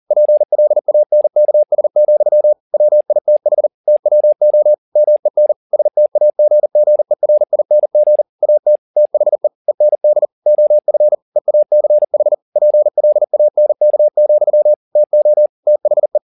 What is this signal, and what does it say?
Morse code, heard as a keyed tone, about 40 wpm, 600 Hz. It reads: PLANKS, WITH TWO MEN STAGGERING AT THE END OF EACH PLANK, TO THE